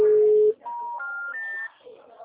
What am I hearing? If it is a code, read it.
Special information tone, with speech.